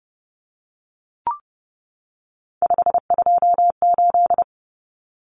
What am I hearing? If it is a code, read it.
Morse code, heard as a keyed tone, about 30 wpm, 700 Hz. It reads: 528
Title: Numbers - 3-Digit 30wpm